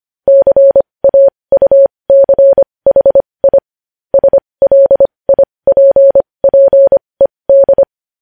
Morse code, heard as a keyed tone, about 25 wpm, spaced as for 20 wpm, 550 Hz. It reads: CAUCHI SLIPPED
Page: news-headlines-in-morse-code-at-20-wpm